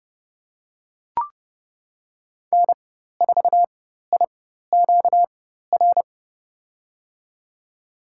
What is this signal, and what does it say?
Morse code, heard as a keyed tone, about 32 wpm, spaced as for 14 wpm, 700 Hz. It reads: N4IQR
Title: USA Call Signs - 5 Characters - 4x Character Spacing - 30wpm